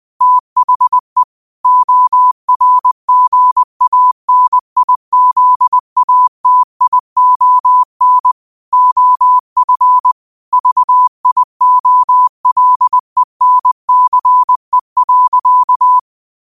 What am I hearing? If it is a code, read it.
Morse code, heard as a keyed tone, about 20 wpm, 1000 Hz. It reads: THE ORGANIZATION OF VIOLENCE.
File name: qotdfiles / 20wpm